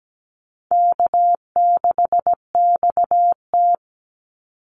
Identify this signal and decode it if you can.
Morse code, heard as a keyed tone, about 17 wpm, 700 Hz. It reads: K6XT
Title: Easy Call Signs - 17wpm